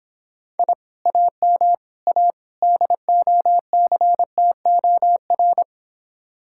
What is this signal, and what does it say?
Morse code, heard as a keyed tone, about 26 wpm, 700 Hz. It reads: I AM A DOCTOR